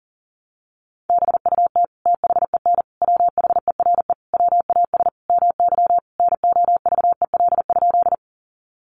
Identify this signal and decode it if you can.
Morse code, heard as a keyed tone, about 40 wpm, 700 Hz. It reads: BUT THEN WHERE WAS MY NOVEL?
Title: Sentences from Top 1300 Words 40wpm